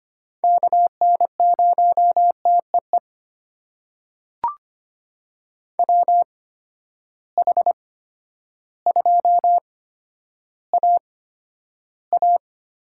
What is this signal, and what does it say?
Morse code, heard as a keyed tone, about 25 wpm, spaced as for 11 wpm, 700 Hz. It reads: KN0TEE WH2AA
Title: USA Call Signs - 8x Character Spacing - 25wpm